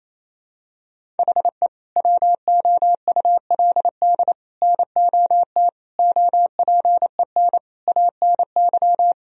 Morse code, heard as a keyed tone, about 27 wpm, 700 Hz. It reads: HE WOULD NOT OPEN ANY
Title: Sentences from Top 1000 Words - Review Part 1 - 28wpm